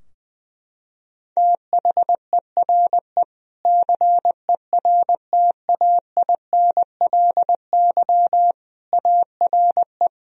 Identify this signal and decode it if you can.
Morse code, heard as a keyed tone, about 20 wpm, 700 Hz. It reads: THERE CERTAINLY ARE
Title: Sentences from Top 1000 Words - Review Part 3 - 20wpm